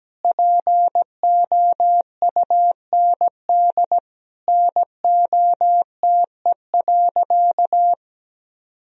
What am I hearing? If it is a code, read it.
Morse code, heard as a keyed tone, about 17 wpm, 700 Hz. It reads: POUND NOTE.
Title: Sentences from Top 1000 Words - Review Part 2 - 17wpm